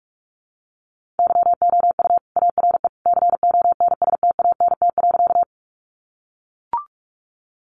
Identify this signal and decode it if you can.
Morse code, heard as a keyed tone, about 44 wpm, 700 Hz. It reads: YOU ARE CONSTANT.
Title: Sentences from Top 1500 Words 45wpm